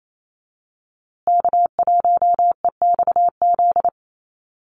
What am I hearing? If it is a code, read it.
Morse code, heard as a keyed tone, about 28 wpm, 700 Hz. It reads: K1EXZ
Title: USA Call Signs Repeated 2x 28wpm